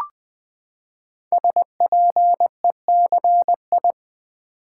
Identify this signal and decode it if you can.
Morse code, heard as a keyed tone, about 20 wpm, 700 Hz. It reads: SPECI
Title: Build A Word - 5 Letters At A Time - 10-Letter Words - 20wpm